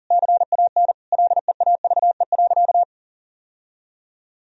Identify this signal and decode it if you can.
Morse code, heard as a keyed tone, about 37 wpm, 700 Hz. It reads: CAN LEAVE.
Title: Sentences from Top 1000 Words - Review Part 4 - 40wpm